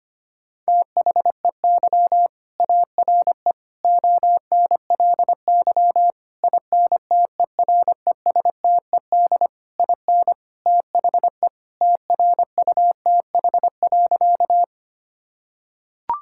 Morse code, heard as a keyed tone, about 25 wpm, 700 Hz. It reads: THEY ARE ONLY INTERESTED IN THE TRUTH.
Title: Sentences from Top 1000 Words - Review Part 6 - 25wpm